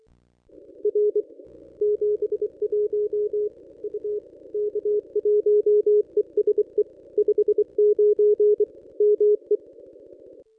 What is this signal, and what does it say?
Morse code, heard as a keyed tone, about 25 wpm, 420 Hz. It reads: R 71 U K1ESE 59 ME
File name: contests / ss-cw-k5tr-op-n5rz -1000